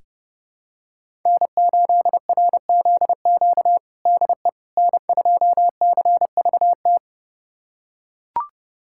Morse code, heard as a keyed tone, about 30 wpm, 700 Hz. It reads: N8RZQ DE N2CVT
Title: USA Callsign DE Callsign - 5 Characters - 30wpm